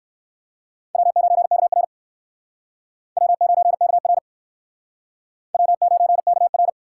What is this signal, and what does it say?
Morse code, heard as a keyed tone, about 56 wpm, 700 Hz. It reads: W9KR W9KR W9KR
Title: Speed Racing - USA 1x2 Call Signs 45wpm